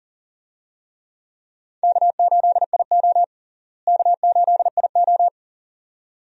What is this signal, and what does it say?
Morse code, heard as a keyed tone, about 37 wpm, 700 Hz. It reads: K8IO K8IO
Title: USA Call Signs - 4 Characters - Repeated 2x - 40wpm